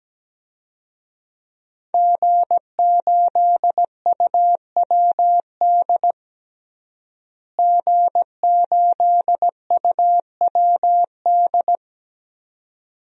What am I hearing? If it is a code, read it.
Morse code, heard as a keyed tone, about 17 wpm, 700 Hz. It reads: G8UWD G8UWD
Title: DXCC-100 Call Signs - 5 Characters - Repeated 2x - 17wpm